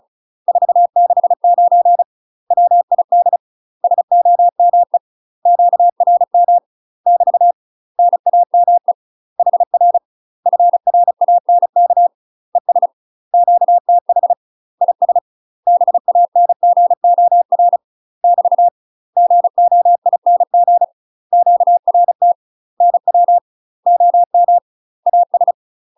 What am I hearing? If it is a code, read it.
Morse code, heard as a keyed tone, about 34 wpm, 700 Hz. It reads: 469 WID SOME QRM <BT> NAME HR FRANK ES QTH IS BANGOR <BT> GOING QRT NW OM AS
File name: UK MASTER QSOs TRAINING / 35WPM / UK MASTER QSOs 0036